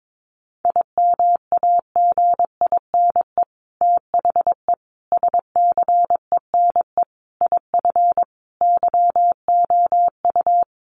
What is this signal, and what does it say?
Morse code, heard as a keyed tone, about 22 wpm, 700 Hz. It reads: IMAGINE THE SCENE IF YOU
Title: Sentences from Top 1400 Words 22wpm